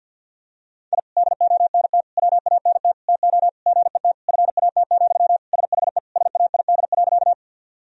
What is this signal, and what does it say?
Morse code, heard as a keyed tone, about 50 wpm, 700 Hz. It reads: I DONT WANT TO GET FAT, SHE SAID.
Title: Sentences from Top 1000 Words - Review Part 4 - 50wpm